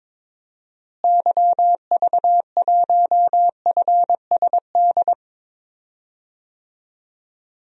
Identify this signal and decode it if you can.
Morse code, heard as a keyed tone, about 22 wpm, 700 Hz. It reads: YV1FSD
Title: DXCC-100 Call Signs - 6 Characters - Repeated 2x - 22wpm